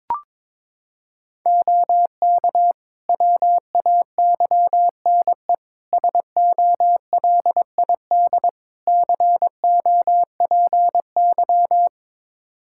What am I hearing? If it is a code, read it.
Morse code, heard as a keyed tone, about 22 wpm, 700 Hz. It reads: OK WAYNE SOLID COPY
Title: Practice QSO - Exchange 3 - More Info - 22wpm